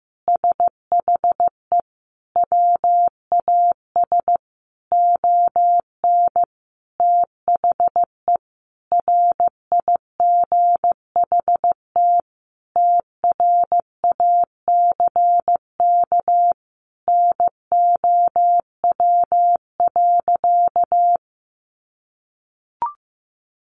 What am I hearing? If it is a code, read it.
Morse code, heard as a keyed tone, about 15 wpm, 700 Hz. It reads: SHE WAS ON THE RIGHT TRACK NOW.
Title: Sentences from Top 1100 Words 15wpm